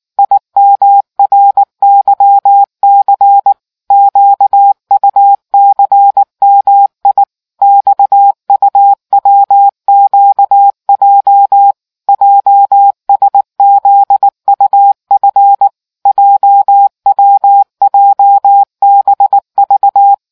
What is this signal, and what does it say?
Morse code, heard as a keyed tone, about 19 wpm, 800 Hz. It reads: IMRYC QUCMI XUWQJ JSZUF JWJBV